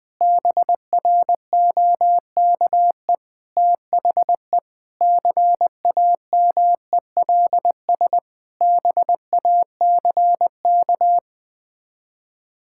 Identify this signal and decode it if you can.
Morse code, heard as a keyed tone, about 20 wpm, 700 Hz. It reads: BROKE THE CAMELS BACK
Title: Sets of 8 Words - 20wpm